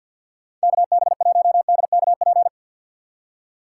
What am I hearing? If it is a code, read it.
Morse code, heard as a keyed tone, about 52 wpm, 700 Hz. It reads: KB1DKP
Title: USA Call Signs - 6 Characters - 6x Character Spacing - 50wpm